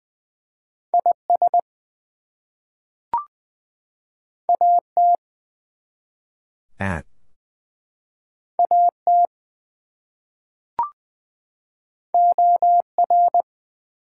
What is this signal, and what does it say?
Morse code, heard as a keyed tone, about 20 wpm, 700 Hz. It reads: IS AT AT OR
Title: Two Letter Words - 20wpm